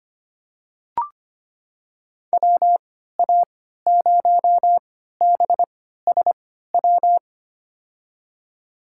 Morse code, heard as a keyed tone, about 25 wpm, spaced as for 14 wpm, 700 Hz. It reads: WA0BSW
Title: USA Call Signs - 6 Characters - 3x Character Spacing - 25wpm